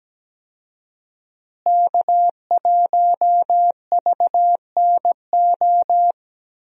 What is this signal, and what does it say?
Morse code, heard as a keyed tone, about 17 wpm, 700 Hz. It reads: K1VNO
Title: USA Call Signs - 5 Characters - Repeated 2x - 17wpm